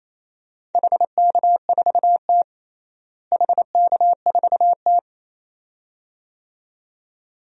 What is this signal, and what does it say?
Morse code, heard as a keyed tone, about 28 wpm, 700 Hz. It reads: HK4T HK4T
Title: DXCC-100 Call Signs - 4 Characters - Repeated 2x - 28wpm